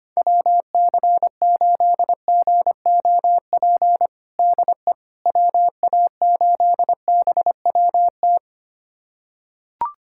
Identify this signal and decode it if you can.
Morse code, heard as a keyed tone, about 25 wpm, 700 Hz. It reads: WC8GOP DE WA8BWT
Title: USA Callsign DE Callsign 25wpm